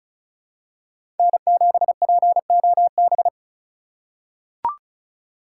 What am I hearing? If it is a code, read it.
Morse code, heard as a keyed tone, about 36 wpm, 700 Hz. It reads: N7POB
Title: USA Call Signs - 5 Characters - 3x Character Spacing - 35wpm